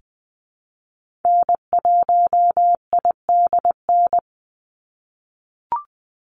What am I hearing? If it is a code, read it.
Morse code, heard as a keyed tone, about 20 wpm, 700 Hz. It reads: N1IDN
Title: NA SOTA Activator Callsigns 20wpm